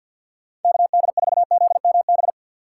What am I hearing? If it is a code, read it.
Morse code, heard as a keyed tone, about 46 wpm, 700 Hz. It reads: KD4ZMB
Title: USA Call Signs - 6 Characters - 3x Character Spacing - 50wpm